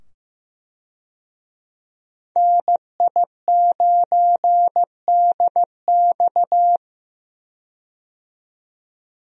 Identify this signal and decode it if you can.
Morse code, heard as a keyed tone, about 15 wpm, 700 Hz. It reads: NI9DX
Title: USA Call Signs - 5 Characters - 15wpm